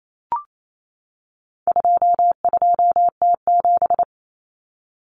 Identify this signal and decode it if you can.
Morse code, heard as a keyed tone, about 28 wpm, 700 Hz. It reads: 22T7
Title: Common Cut Numbers - 4-Digit 28wpm